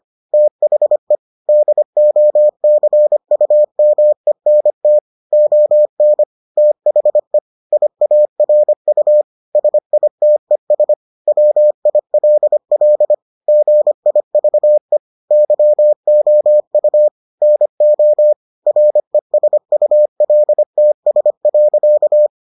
Morse code, heard as a keyed tone, about 25 wpm, 600 Hz. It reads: THE DOCUMENT ON THE IARU SITES WILL GIVE YOU NO RESULTS.